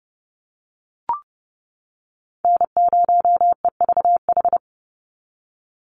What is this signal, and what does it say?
Morse code, heard as a keyed tone, about 30 wpm, 700 Hz. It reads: N0EVH
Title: NA SOTA Chaser Callsigns 30wpm